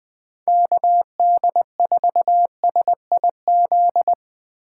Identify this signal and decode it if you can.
Morse code, heard as a keyed tone, about 22 wpm, 700 Hz. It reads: KD4SIZ
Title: Speed Racing - USA 2x3 Call Signs 20wpm